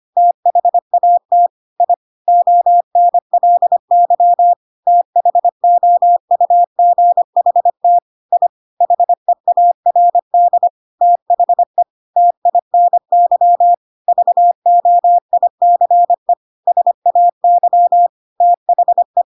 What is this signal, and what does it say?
Morse code, heard as a keyed tone, about 25 wpm, 700 Hz. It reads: THAT I ONLY THOUGHT I HEARD THE TINY VOICE SAY THE
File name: Pinocchio / Pinocchio ch1 25 25 ik1hge